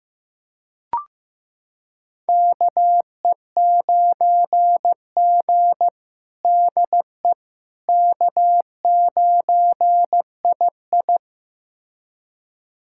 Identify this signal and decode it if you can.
Morse code, heard as a keyed tone, about 15 wpm, 700 Hz. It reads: KE9G DE K9II
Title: USA Callsign DE Callsign - 4 Characters - 15wpm